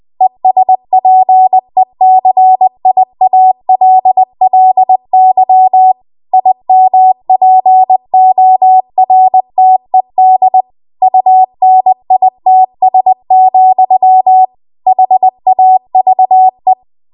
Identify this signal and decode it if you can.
Morse code, heard as a keyed tone, about 20 wpm, 750 Hz. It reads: ESPECIALLY IMPORTED UNITS, HAVE
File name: Archive / 20 WPM / 20WPM